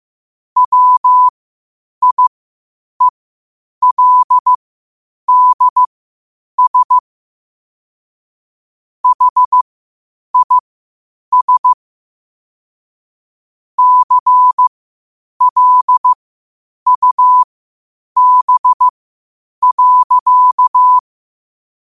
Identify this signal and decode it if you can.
Morse code, heard as a keyed tone, about 15 wpm, spaced as for 8 wpm, 1000 Hz. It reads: WIELDS HIS CLUB.
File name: qotdfiles / 8wpm